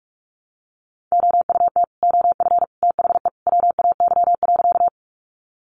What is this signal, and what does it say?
Morse code, heard as a keyed tone, about 44 wpm, 700 Hz. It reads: OUT OF THE WAY.
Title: Sentences from Top 100 Words 45wpm